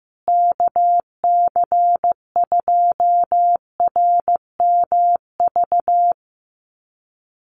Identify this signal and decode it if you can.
Morse code, heard as a keyed tone, about 14 wpm, 700 Hz. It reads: KC2RMV
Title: USA Call Signs Repeated 3x 15wpm